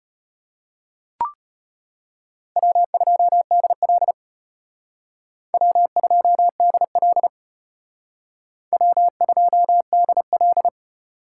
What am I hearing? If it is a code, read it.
Morse code, heard as a keyed tone, about 33 wpm, 700 Hz. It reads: W2DL W2DL W2DL